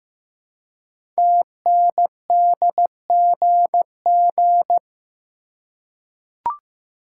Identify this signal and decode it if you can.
Morse code, heard as a keyed tone, about 15 wpm, 700 Hz. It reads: TNDGG
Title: Code Groups - 5 Letters - 15wpm